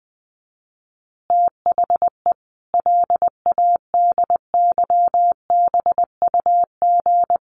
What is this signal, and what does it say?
Morse code, heard as a keyed tone, about 19 wpm, 700 Hz. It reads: THE LADYBUG
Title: Additional Easy Phrases 20wpm